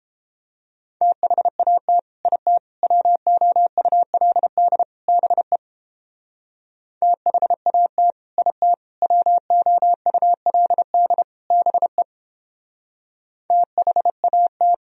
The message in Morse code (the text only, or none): THAT IT WOULD BE THAT IT WOULD BE THAT